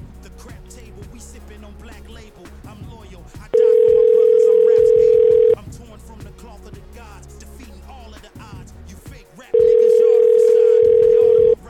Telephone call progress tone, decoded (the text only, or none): ringback tone